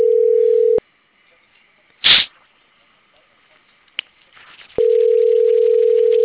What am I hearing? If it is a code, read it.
Ringback tone (North America), 440 + 480 Hz.
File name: CENTRAL CAEA / out agcaea02 - .46073